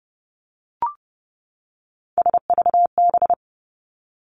Morse code, heard as a keyed tone, about 30 wpm, 700 Hz. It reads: SVB
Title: Three Challenging Letter-Number Combinations 30wpm